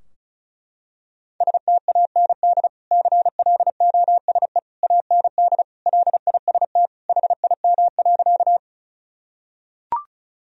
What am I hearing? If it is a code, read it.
Morse code, heard as a keyed tone, about 36 wpm, 700 Hz. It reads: STAND CLOSE AND LIST HIM.
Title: Sentences from Top 1000 Words - Review Part 2 - 35wpm